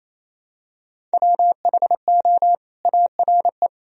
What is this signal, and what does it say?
Morse code, heard as a keyed tone, about 28 wpm, 700 Hz. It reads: WHO ARE
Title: Sentences from Top 1000 Words - Review Part 1 - 28wpm